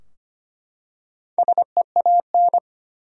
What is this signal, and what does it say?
Morse code, heard as a keyed tone, about 25 wpm, 700 Hz. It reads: SEAN
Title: USA Common Names - Max 4 Letters - 25wpm